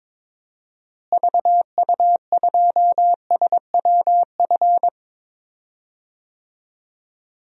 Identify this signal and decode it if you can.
Morse code, heard as a keyed tone, about 22 wpm, 700 Hz. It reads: VU2SWF
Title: DXCC-100 Call Signs - 6 Characters - Repeated 2x - 22wpm